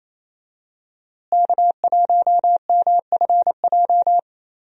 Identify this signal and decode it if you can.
Morse code, heard as a keyed tone, about 28 wpm, 700 Hz. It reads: K1MFJ
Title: USA Call Signs - 5 Characters - Repeated 3x - 28wpm